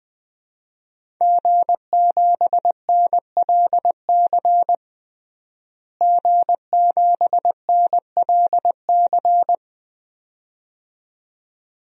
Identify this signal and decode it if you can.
Morse code, heard as a keyed tone, about 20 wpm, 700 Hz. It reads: G7NLC G7NLC
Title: DXCC-100 Call Signs - 5 Characters - Repeated 2x - 20wpm